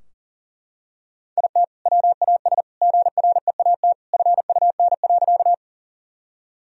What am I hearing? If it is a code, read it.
Morse code, heard as a keyed tone, about 40 wpm, 700 Hz. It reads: IT WAS GREAT FUN.
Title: Sentences from Top 1000 Words - Review Part 4 - 40wpm